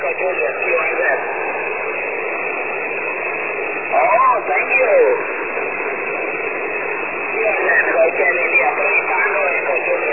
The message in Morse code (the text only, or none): none